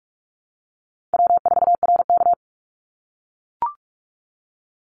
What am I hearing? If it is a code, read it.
Morse code, heard as a keyed tone, about 44 wpm, 700 Hz. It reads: W4RK
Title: Easy Call Signs - 45wpm